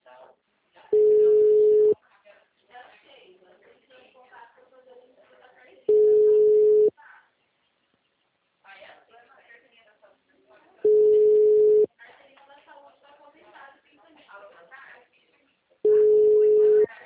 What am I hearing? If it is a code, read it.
Ringback tone (Europe, 425 Hz), with speech.
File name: out- -2009- - - .59784